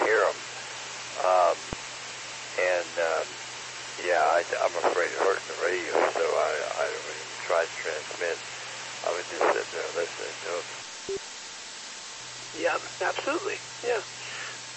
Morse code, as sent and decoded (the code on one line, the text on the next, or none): none